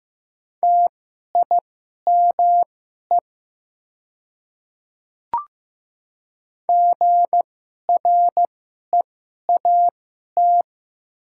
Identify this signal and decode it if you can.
Morse code, heard as a keyed tone, about 15 wpm, spaced as for 11 wpm, 700 Hz. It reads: TIME GREAT
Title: 2x Character Spacing - Top 200 Words - 15wpm